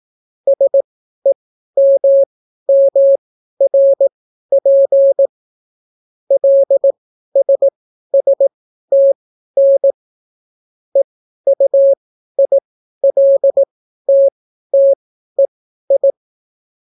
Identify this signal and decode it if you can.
Morse code, heard as a keyed tone, about 18 wpm, spaced as for 12 wpm, 550 Hz. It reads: SEMMRP LSSTN EUILTTEI